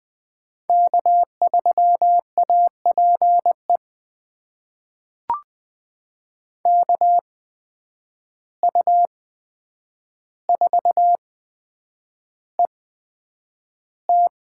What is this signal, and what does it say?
Morse code, heard as a keyed tone, about 21 wpm, spaced as for 9 wpm, 700 Hz. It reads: K3APE KU4ET
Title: USA Call Signs - 5 Characters - 8x Character Spacing - 20wpm